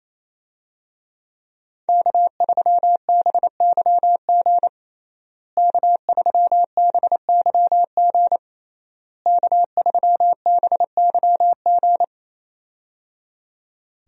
Morse code, heard as a keyed tone, about 27 wpm, 700 Hz. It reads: K3BYG K3BYG K3BYG
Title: USA Call Signs - 5 Characters - Repeated 3x - 28wpm